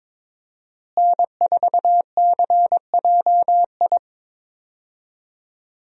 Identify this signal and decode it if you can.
Morse code, heard as a keyed tone, about 22 wpm, 700 Hz. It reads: N4CJI
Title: USA Call Signs - 5 Characters - Repeated 2x - 22wpm